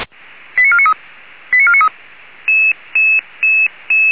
Five-tone selective call (ZVEI), heard as quick runs of five tones, tones of about 70 ms. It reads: 89493 89493
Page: SDR/ZVEI